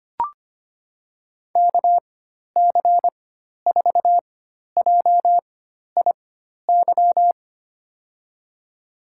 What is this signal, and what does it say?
Morse code, heard as a keyed tone, about 25 wpm, spaced as for 12 wpm, 700 Hz. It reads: KC4JIY
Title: USA Call Signs - 6 Characters - 4x Character Spacing - 25wpm